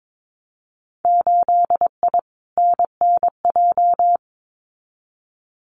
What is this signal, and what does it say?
Morse code, heard as a keyed tone, about 22 wpm, 700 Hz. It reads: 8I NNJ
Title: Winter Field Day Exchanges Encore 22wpm